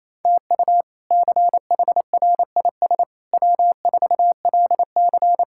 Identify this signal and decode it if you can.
Morse code, heard as a keyed tone, about 28 wpm, 700 Hz. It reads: TU CHRIS W4LC
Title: K1USN SST QSO Practice - 28wpm